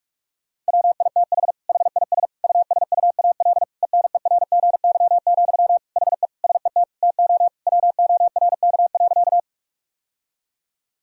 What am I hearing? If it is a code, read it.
Morse code, heard as a keyed tone, about 46 wpm, 700 Hz. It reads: WITH HIS USUAL ENERGY, HE SET TO WORK.
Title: Sentences from Top 1000 Words - Review Part 6 - 45wpm